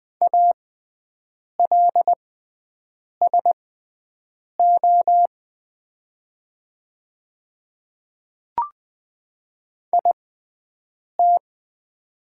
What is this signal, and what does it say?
Morse code, heard as a keyed tone, about 20 wpm, spaced as for 7 wpm, 690 Hz. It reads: ALSO IT
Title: 6x Character Spacing - Top 200 Words - 20wpm